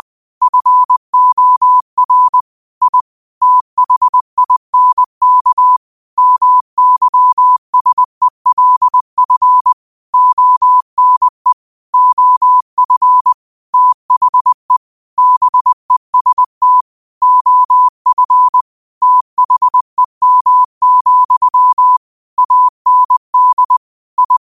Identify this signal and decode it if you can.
Morse code, heard as a keyed tone, about 20 wpm, 1000 Hz. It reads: FOR I THINK MYSELF ONE OF THE BEST OF THEM, AND I
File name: 20wpm